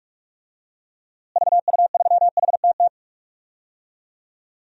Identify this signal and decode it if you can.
Morse code, heard as a keyed tone, about 41 wpm, 700 Hz. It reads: VU3HTT
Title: DXCC-100 Call Signs - 6 Characters - Repeated 2x - 45wpm